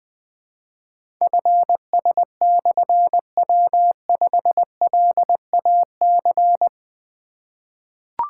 Morse code, heard as a keyed tone, about 20 wpm, 700 Hz. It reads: FS/W5LAC
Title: Difficult Call Signs 20wpm